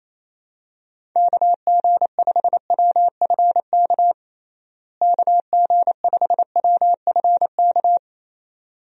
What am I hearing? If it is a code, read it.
Morse code, heard as a keyed tone, about 28 wpm, 700 Hz. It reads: KG5WFK KG5WFK
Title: USA Call Signs - 6 Characters - Repeated 2x - 28wpm